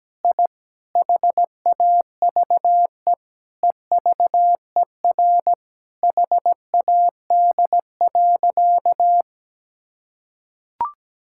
Morse code, heard as a keyed tone, about 17 wpm, 700 Hz. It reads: I HAVE EVER HAD.
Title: Sentences from Top 1000 Words - Review Part 6 - 17wpm